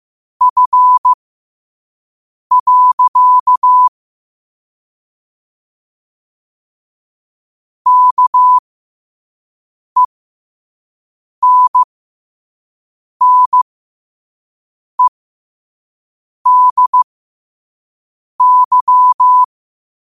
Morse code, heard as a keyed tone, about 15 wpm, spaced as for 5 wpm, 1000 Hz. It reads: F. KENNEDY
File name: qotdfiles / 5wpm